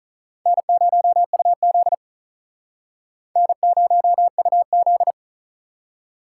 Morse code, heard as a keyed tone, about 44 wpm, 700 Hz. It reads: N0UZ N0UZ
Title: Speed Racing - USA Call Signs 35wpm